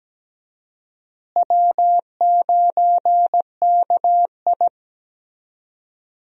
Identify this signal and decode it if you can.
Morse code, heard as a keyed tone, about 17 wpm, 700 Hz. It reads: W9KI
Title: USA Call Signs - 4 Characters - Repeated 3x - 17wpm